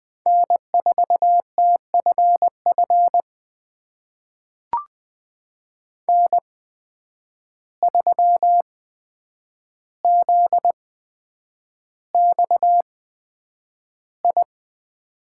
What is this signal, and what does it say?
Morse code, heard as a keyed tone, about 20 wpm, spaced as for 9 wpm, 700 Hz. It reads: N4TFF N3ZXI
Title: USA Call Signs - 5 Characters - 8x Character Spacing - 20wpm